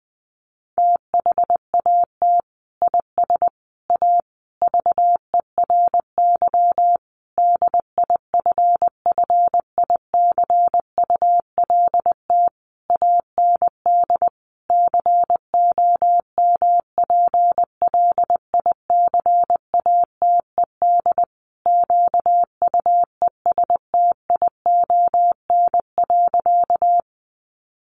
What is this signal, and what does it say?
Morse code, heard as a keyed tone, about 20 wpm, 700 Hz. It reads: THAT IS A VERY DIFFICULT AND COMPLICATED QUESTION.
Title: Sentences from Top 1200 Words 20wpm